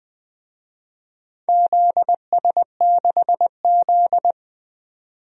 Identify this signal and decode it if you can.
Morse code, heard as a keyed tone, about 21 wpm, 700 Hz. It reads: ZS6Z
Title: DXCC-100 Call Signs - 4 Characters - 20wpm